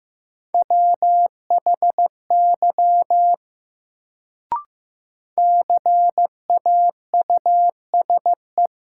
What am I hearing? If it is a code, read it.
Morse code, heard as a keyed tone, about 15 wpm, 700 Hz. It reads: WHY CAUSE
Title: Sentences from Top 200 Words - New Word Drill - 15wpm